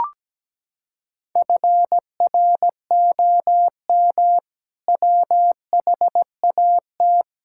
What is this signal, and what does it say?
Morse code, heard as a keyed tone, about 17 wpm, 700 Hz. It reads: FROM WHAT
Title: Sentences from Top 1000 Words - Review Part 6 - 17wpm